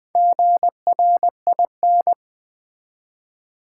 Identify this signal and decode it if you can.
Morse code, heard as a keyed tone, about 20 wpm, 700 Hz. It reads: GRIN
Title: Build A Word - 1 Letter At A Time - 4-Letter Words - 20wpm